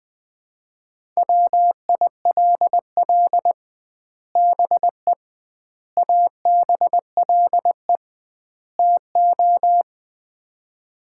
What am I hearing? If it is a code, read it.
Morse code, heard as a keyed tone, about 20 wpm, 700 Hz. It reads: WILL BE ABLE TO
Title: 2x Word-Spacing - Sets of 4 Words - 20wpm